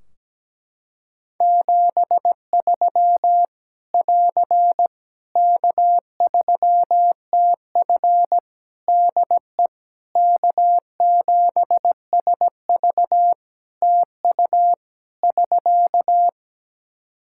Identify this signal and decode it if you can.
Morse code, heard as a keyed tone, about 17 wpm, 700 Hz. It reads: 73 <AR> K3TF DE K7SV TU <SK>
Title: Practice QSO - Exchange 4 - Ending - 17wpm